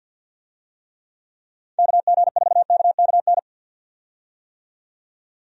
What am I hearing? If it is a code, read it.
Morse code, heard as a keyed tone, about 51 wpm, 700 Hz. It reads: KG4KKN
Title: USA Call Signs - 6 Characters - 50wpm